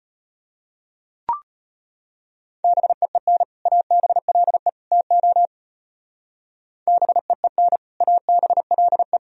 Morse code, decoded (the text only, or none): BEEN ABLE TO BEEN ABLE